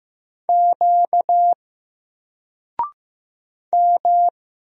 Morse code, heard as a keyed tone, about 15 wpm, 700 Hz. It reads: Q M
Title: Single Letter-Number 15wpm